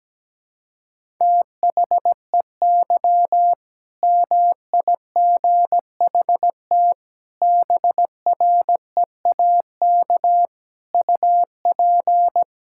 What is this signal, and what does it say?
Morse code, heard as a keyed tone, about 17 wpm, 700 Hz. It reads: THEY MIGHT BREAK UP
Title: Sentences from Top 1000 Words - Review Part 6 - 17wpm